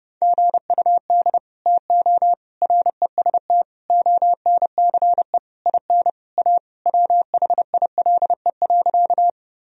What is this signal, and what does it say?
Morse code, heard as a keyed tone, about 30 wpm, 700 Hz. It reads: GUD TO REST ONCE IN A WHILE.
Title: Sentences from Top 500 Words and Abbreviations 30wpm